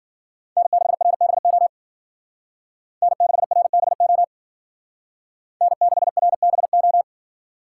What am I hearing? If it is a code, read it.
Morse code, heard as a keyed tone, about 52 wpm, 700 Hz. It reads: N6RBO N6RBO N6RBO